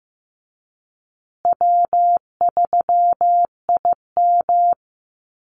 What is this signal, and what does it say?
Morse code, heard as a keyed tone, about 15 wpm, 700 Hz. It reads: W3IM
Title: USA Call Signs Repeated 2x 15wpm